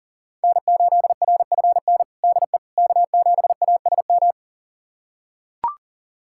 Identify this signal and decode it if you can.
Morse code, heard as a keyed tone, about 40 wpm, 700 Hz. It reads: N8RFN DE K7ASM
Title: USA Callsign DE Callsign - 5 Characters - 40wpm